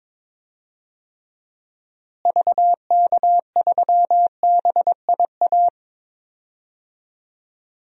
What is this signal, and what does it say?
Morse code, heard as a keyed tone, about 22 wpm, 700 Hz. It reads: VK3BIA
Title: DXCC-100 Call Signs - 6 Characters - Repeated 2x - 22wpm